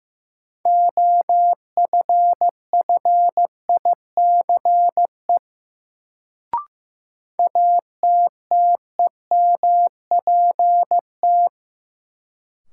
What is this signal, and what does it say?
Morse code, heard as a keyed tone, about 16 wpm, 700 Hz. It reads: OFFICE ATTEMPT
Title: Sentences from Top 1000 Words - New Word Drill - 15wpm